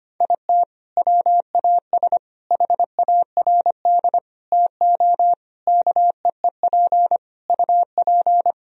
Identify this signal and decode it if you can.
Morse code, heard as a keyed tone, about 25 wpm, 700 Hz. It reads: IT WAS HARD TO KEEP UP
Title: Sentences from Top 1000 Words - Review Part 1 - 25wpm